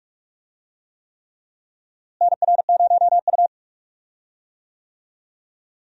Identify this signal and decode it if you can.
Morse code, heard as a keyed tone, about 46 wpm, 700 Hz. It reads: NR0U
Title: USA Call Signs - 4 Characters - Repeated 3x - 45wpm